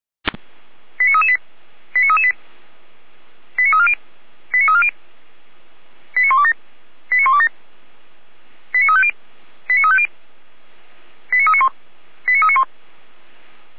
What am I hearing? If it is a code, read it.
Five-tone selective call (ZVEI), heard as quick runs of five tones, tones of about 70 ms. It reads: 89338 89338 89340 89340 89127 89127 89350 89350 89391 89391
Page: audio.f85360.feuerwehr-orlamuende.org